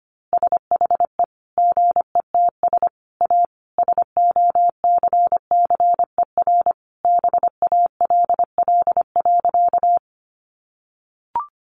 Morse code, heard as a keyed tone, about 25 wpm, 700 Hz. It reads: SHE GETS A SOCCER BALL.
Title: Easy Phrases 25wpm